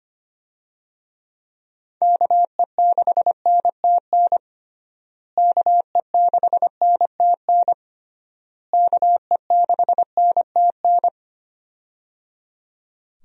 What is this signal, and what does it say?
Morse code, heard as a keyed tone, about 25 wpm, 700 Hz. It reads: KE6NTN KE6NTN KE6NTN
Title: USA Call Signs - 6 Characters - Repeated 3x - 25wpm